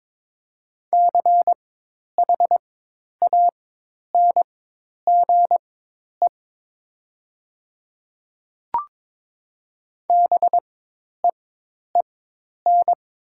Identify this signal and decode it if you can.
Morse code, heard as a keyed tone, about 22 wpm, spaced as for 10 wpm, 700 Hz. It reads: CHANGE BEEN
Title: 4x Character Spacing - Top 500 Words - 22wpm